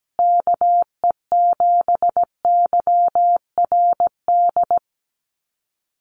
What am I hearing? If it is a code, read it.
Morse code, heard as a keyed tone, about 17 wpm, 700 Hz. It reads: KE7YRD
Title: USA Call Signs Repeated 2x 17wpm